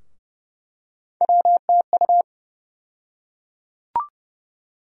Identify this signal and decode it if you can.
Morse code, heard as a keyed tone, about 32 wpm, 700 Hz. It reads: WTU
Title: USA Call Sign Postfixes - 3 Letters 30wpm